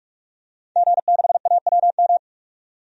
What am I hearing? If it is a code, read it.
Morse code, heard as a keyed tone, about 45 wpm, 700 Hz. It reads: G6AWM
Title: DXCC-100 Call Signs - 5 Characters - 45wpm